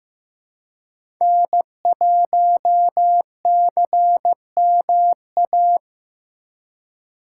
Morse code, heard as a keyed tone, about 15 wpm, 700 Hz. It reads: N1CMA
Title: USA Call Signs - 3x Character Spacing - 15wpm